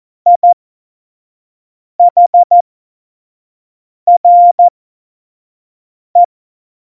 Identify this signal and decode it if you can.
Morse code, heard as a keyed tone, about 14 wpm, spaced as for 5 wpm, 700 Hz. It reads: IHRE
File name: Morsekurs / Witze / Witz 30. Oktober 2018, 25 F70